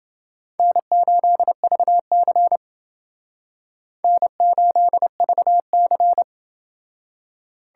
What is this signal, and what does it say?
Morse code, heard as a keyed tone, about 28 wpm, 700 Hz. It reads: N8VC N8VC